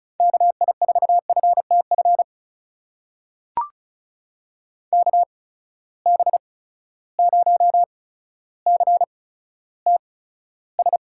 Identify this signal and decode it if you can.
Morse code, heard as a keyed tone, about 36 wpm, spaced as for 15 wpm, 700 Hz. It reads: KI4FTF KB0CTS
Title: USA Call Signs - 6 Characters - 8x Character Spacing - 35wpm